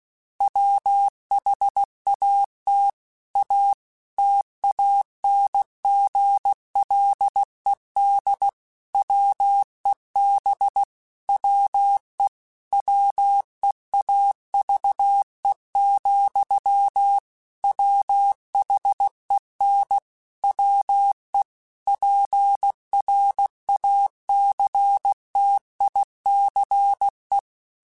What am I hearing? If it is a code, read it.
Morse code, heard as a keyed tone, about 16 wpm, 800 Hz. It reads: WHAT A TANGLED WEB WE WEAVE, WHEN WE PRACTICE
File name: Quotes / Quotes-mp3s / Deceive @15